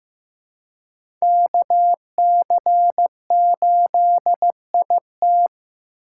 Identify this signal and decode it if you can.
Morse code, heard as a keyed tone, about 15 wpm, 700 Hz. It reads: KC8IT
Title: USA Call Signs - 5 Characters - 6x Character Spacing - 15wpm